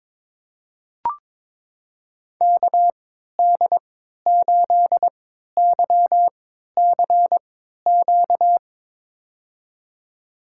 Morse code, heard as a keyed tone, about 22 wpm, spaced as for 13 wpm, 700 Hz. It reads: KD8YCQ
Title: USA Call Signs - 3x Character Spacing - 22wpm